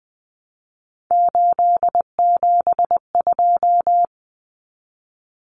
Morse code, heard as a keyed tone, about 20 wpm, 700 Hz. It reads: 872
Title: Common Cut Numbers - 3-Digit 20wpm